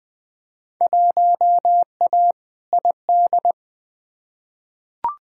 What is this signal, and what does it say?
Morse code, heard as a keyed tone, about 20 wpm, 700 Hz. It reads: 1A ID
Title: Field Day Exchanges Encore - 20wpm